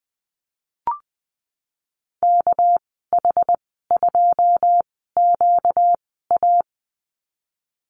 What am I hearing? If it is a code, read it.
Morse code, heard as a keyed tone, about 20 wpm, spaced as for 14 wpm, 700 Hz. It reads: KH2QA
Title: 245 Lesson 39g - Letter z - Call Signs 20wpm